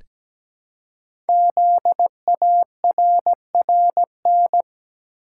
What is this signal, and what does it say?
Morse code, heard as a keyed tone, about 17 wpm, 700 Hz. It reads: ZARRN
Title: Code Groups - 5 Letters - 17wpm